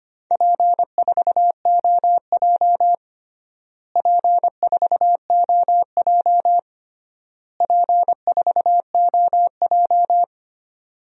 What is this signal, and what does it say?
Morse code, heard as a keyed tone, about 25 wpm, 700 Hz. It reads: P4OJ P4OJ P4OJ
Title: DXCC-100 Call Signs - 4 Characters - Repeated 3x - 25wpm